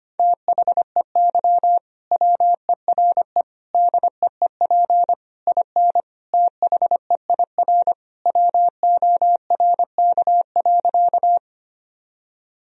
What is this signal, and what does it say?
Morse code, heard as a keyed tone, about 25 wpm, 700 Hz. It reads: THEY WERE DEEP IN THEIR WORK.
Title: Sentences from Top 1000 Words - Review Part 3 - 25wpm